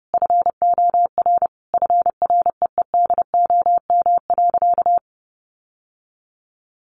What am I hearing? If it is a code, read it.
Morse code, heard as a keyed tone, about 30 wpm, 700 Hz. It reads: FOR FREEDOM.
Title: Sentences from Top 1100 Words 30wpm